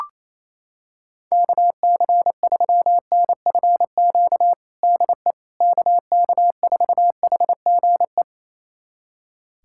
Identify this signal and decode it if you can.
Morse code, heard as a keyed tone, about 28 wpm, 700 Hz. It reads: KC3NFQ DE KK4HGE
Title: USA Callsign DE Callsign - 6 Characters - 28wpm